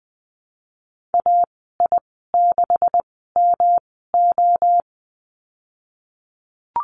Morse code, heard as a keyed tone, about 20 wpm, spaced as for 15 wpm, 700 Hz. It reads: AI6MO
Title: 147 Lesson 24g - Letter y - Call Signs 20wpm